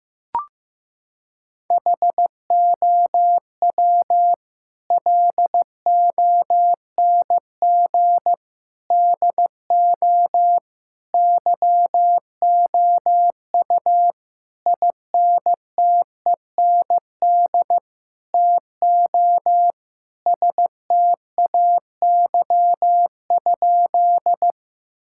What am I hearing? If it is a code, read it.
Morse code, heard as a keyed tone, about 16 wpm, 700 Hz. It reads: HOW LONG DO YOU INTEND TO STAY?
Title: Sentences from Top 1600 Words 15wpm